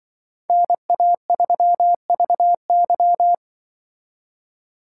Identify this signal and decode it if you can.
Morse code, heard as a keyed tone, about 25 wpm, 700 Hz. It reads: NA3VY
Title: Speed Racing - USA 2x2 Call Signs 20wpm